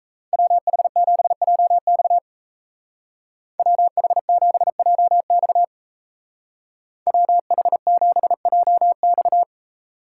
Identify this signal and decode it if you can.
Morse code, heard as a keyed tone, about 36 wpm, 700 Hz. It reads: WH7JX WH7JX WH7JX